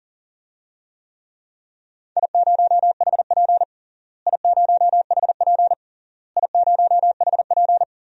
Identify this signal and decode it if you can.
Morse code, heard as a keyed tone, about 40 wpm, 700 Hz. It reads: I0HP I0HP I0HP
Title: DXCC-100 Call Signs - 4 Characters - Repeated 3x - 40wpm